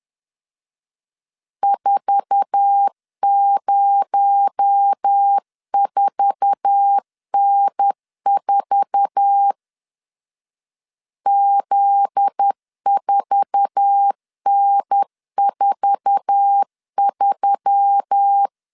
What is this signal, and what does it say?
Morse code, heard as a keyed tone, about 10 wpm, 800 Hz. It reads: 404N4 Z4N43
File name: LEZIONI / 8 YJ45 / 8 YJ4 10wpm 2633